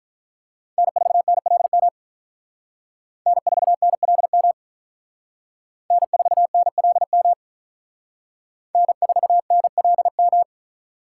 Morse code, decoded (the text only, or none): N4NLM N4NLM N4NLM N4NLM